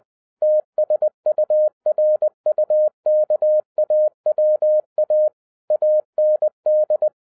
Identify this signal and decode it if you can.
Morse code, heard as a keyed tone, about 20 wpm, 610 Hz. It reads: TSURUKAWA AND